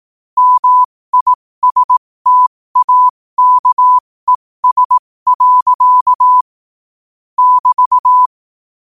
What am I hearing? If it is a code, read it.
Morse code, heard as a keyed tone, about 18 wpm, spaced as for 15 wpm, 1000 Hz. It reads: MISTAKES. <BT>
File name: qotdfiles / 15wpm